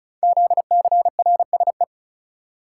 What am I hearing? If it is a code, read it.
Morse code, heard as a keyed tone, about 36 wpm, 700 Hz. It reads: ZCRSE
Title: Code Groups - 5 Letters - 35wpm